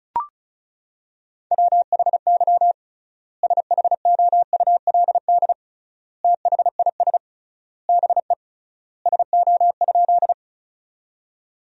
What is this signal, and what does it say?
Morse code, heard as a keyed tone, about 36 wpm, 700 Hz. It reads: WHY SHOULD THIS BE SO?
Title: 3x Word-Spacing - Sentences from Top 300 Words - 35wpm